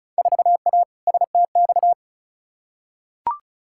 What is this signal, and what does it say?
Morse code, heard as a keyed tone, about 36 wpm, 700 Hz. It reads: 4A STX
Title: Field Day Exchanges Encore - 35wpm